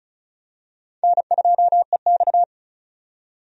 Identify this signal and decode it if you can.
Morse code, heard as a keyed tone, about 36 wpm, 700 Hz. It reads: N2EX
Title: USA Call Signs - 4 Characters - 35wpm